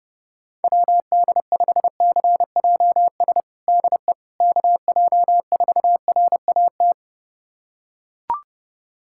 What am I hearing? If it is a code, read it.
Morse code, heard as a keyed tone, about 29 wpm, 700 Hz. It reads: WD5CJS DE KJ4RAT
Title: USA Callsign DE Callsign 30wpm